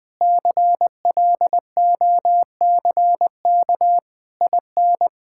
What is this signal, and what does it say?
Morse code, heard as a keyed tone, about 20 wpm, 700 Hz. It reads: CLOCK IN
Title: Pangram Challenge v2 - 20wpm